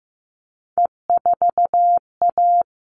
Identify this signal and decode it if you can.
Morse code, heard as a keyed tone, about 15 wpm, 700 Hz. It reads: E4A
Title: All Cut Numbers - 3-Digit 15wpm